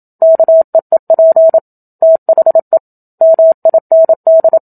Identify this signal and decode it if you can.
Morse code, heard as a keyed tone, about 27 wpm, 650 Hz. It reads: KEEP THE MIND
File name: Book Two / 01-Chapter 1 - 27 WPM